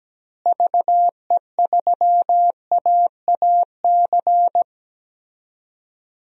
Morse code, heard as a keyed tone, about 17 wpm, 700 Hz. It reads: VE3AAC
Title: DXCC-100 Call Signs - 6 Characters - Repeated 3x - 17wpm